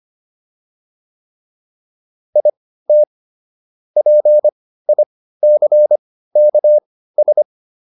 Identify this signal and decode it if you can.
Morse code, heard as a keyed tone, about 25 wpm, spaced as for 15 wpm, 600 Hz. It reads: IT PICKS